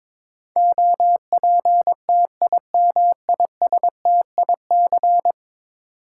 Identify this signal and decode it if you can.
Morse code, heard as a keyed tone, about 22 wpm, 700 Hz. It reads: OPTIMISTIC
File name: practice / 22wpm / 10 Letter Words - First 3 Letters Given - 22wpm